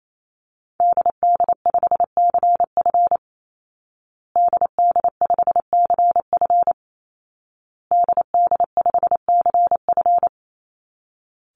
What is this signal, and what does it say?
Morse code, heard as a keyed tone, about 28 wpm, 700 Hz. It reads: DD5CF DD5CF DD5CF
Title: Call Signs Repeated 2x - 28wpm